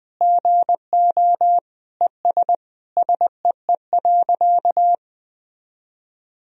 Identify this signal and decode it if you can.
Morse code, heard as a keyed tone, about 20 wpm, 700 Hz. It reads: GO ES SEE.
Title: Sentences from Top 100 Words and Abbreviations 20wpm